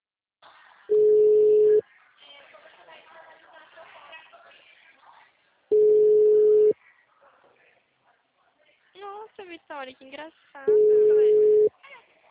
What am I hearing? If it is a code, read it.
Ringback tone (Europe, 425 Hz), with speech.
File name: out- -2016- - - .54283